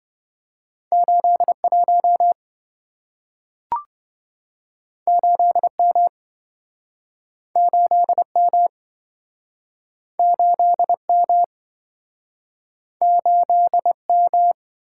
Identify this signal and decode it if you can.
Morse code, heard as a keyed tone, about 27 wpm, 700 Hz. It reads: 81 8M 8M 8M 8M